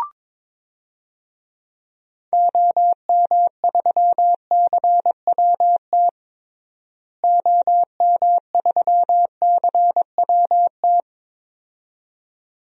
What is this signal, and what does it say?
Morse code, heard as a keyed tone, about 22 wpm, 700 Hz. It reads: OM3CWT OM3CWT
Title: DXCC-100 Call Signs - 6 Characters - Repeated 2x - 22wpm